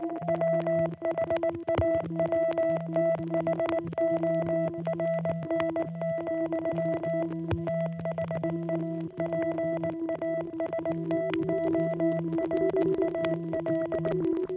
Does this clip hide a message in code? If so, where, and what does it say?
Yes: Morse code from 0.3 to 14.1 s, heard as a keyed tone, about 19 wpm, 660 Hz, with noise. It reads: WHAWTHORN, THE PASTORAL